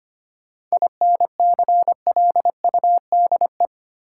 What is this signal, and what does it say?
Morse code, heard as a keyed tone, about 25 wpm, 700 Hz. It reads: INCLUDE
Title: Sentences from Top 1000 Words - Review Part 3 - New Word Drill - 25wpm